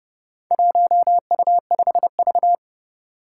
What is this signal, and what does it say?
Morse code, heard as a keyed tone, about 30 wpm, 700 Hz. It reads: 1U5V
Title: Four Challenging Letter-Number Combinations 30wpm